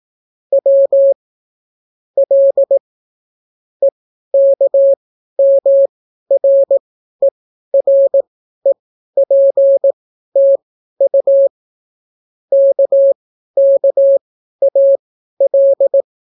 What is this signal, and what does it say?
Morse code, heard as a keyed tone, about 18 wpm, spaced as for 12 wpm, 550 Hz. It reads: W L EKMREREPTU KKAL